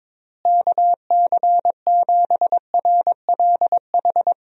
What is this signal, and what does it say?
Morse code, heard as a keyed tone, about 22 wpm, 700 Hz. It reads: KC7RLH
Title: USA Call Signs - 6 Characters - Repeated 3x - 22wpm